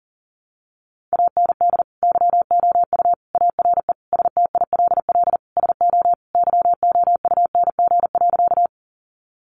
Morse code, heard as a keyed tone, about 40 wpm, 700 Hz. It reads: AND YOU ARE STILL SO YOUNG.
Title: Sentences from Top 400 Words 40wpm